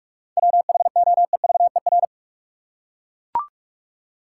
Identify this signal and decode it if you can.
Morse code, heard as a keyed tone, about 41 wpm, 700 Hz. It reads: WHOEVER
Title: Seven Letter Words - 45wpm